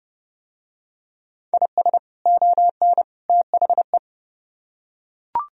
Morse code, heard as a keyed tone, about 30 wpm, 700 Hz. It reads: IS ON THE